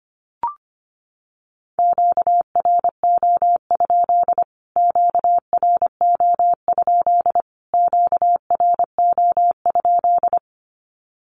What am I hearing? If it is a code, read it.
Morse code, heard as a keyed tone, about 25 wpm, 700 Hz. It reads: QRO? QRO? QRO?
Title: Q-Codes Repeated 3x 25wpm